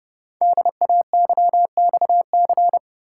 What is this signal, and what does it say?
Morse code, heard as a keyed tone, about 30 wpm, 700 Hz. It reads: DAYXC
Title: Code Groups - 5 Letters - 30wpm